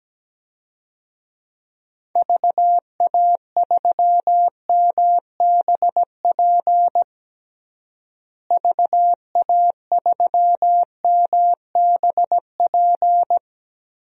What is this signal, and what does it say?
Morse code, heard as a keyed tone, about 17 wpm, 700 Hz. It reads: VA3MBP VA3MBP
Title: DXCC-100 Call Signs - 6 Characters - Repeated 2x - 17wpm